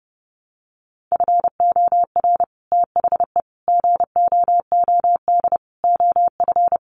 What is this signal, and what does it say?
Morse code, heard as a keyed tone, about 30 wpm, 700 Hz. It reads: FOR THE GOOD OF
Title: Sets of 5 Words Encore2 30wpm